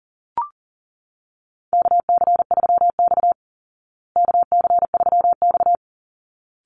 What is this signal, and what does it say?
Morse code, heard as a keyed tone, about 38 wpm, 700 Hz. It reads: KC3X KC3X
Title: Call Signs Repeated 3x - 40wpm